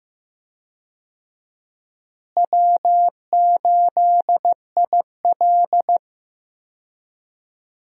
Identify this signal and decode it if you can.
Morse code, heard as a keyed tone, about 15 wpm, 700 Hz. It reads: W8IL
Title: USA Call Signs - 4 Characters - Repeated 2x - 15wpm